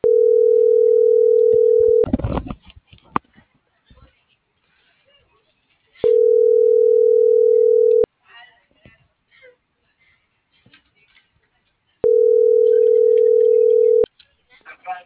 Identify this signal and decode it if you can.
Ringback tone (North America), 440 + 480 Hz.